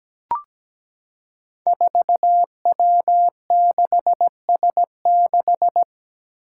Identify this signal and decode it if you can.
Morse code, heard as a keyed tone, about 17 wpm, 700 Hz. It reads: 4W6S6
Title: Five Challenging Letter-Number Combinations 17wpm